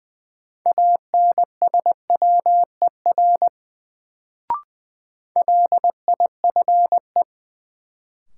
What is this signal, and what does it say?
Morse code, heard as a keyed tone, about 20 wpm, 700 Hz. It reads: ANSWER LIFE
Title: Sentences from Top 300 Words - New Word Drill - 20wpm